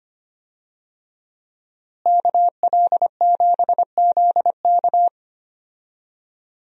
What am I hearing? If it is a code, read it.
Morse code, heard as a keyed tone, about 25 wpm, 700 Hz. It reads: KL7ZK
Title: USA Call Signs - 5 Characters - Repeated 3x - 25wpm